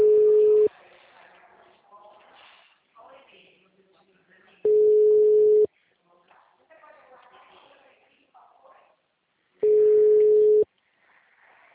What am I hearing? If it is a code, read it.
Ringback tone (Europe, 425 Hz), with speech.
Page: Dracena